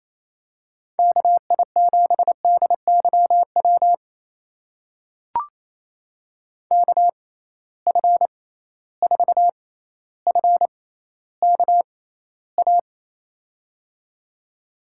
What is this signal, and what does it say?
Morse code, heard as a keyed tone, about 28 wpm, spaced as for 14 wpm, 700 Hz. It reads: KI7DYW KF4FKA
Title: USA Call Signs - 6 Characters - 6x Character Spacing - 28wpm